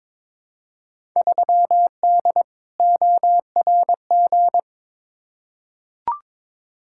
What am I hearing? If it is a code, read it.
Morse code, heard as a keyed tone, about 22 wpm, 700 Hz. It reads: 3D ORG
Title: Field Day Exchanges Encore - 22wpm